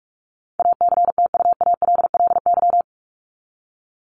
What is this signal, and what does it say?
Morse code, heard as a keyed tone, about 44 wpm, 700 Hz. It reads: ACTUALLY
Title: LY Suffix - Top 100 Words with Base Words 45wpm